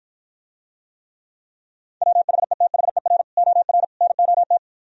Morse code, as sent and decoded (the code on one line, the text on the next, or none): .-- .... - .... .-. / --- .-. / -. --- -
WHETHER OR NOT